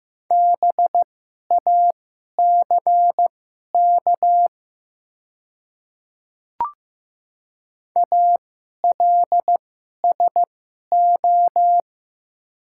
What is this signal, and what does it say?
Morse code, heard as a keyed tone, about 15 wpm, spaced as for 11 wpm, 700 Hz. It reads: BACK ALSO
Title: 2x Character Spacing - Top 200 Words - 15wpm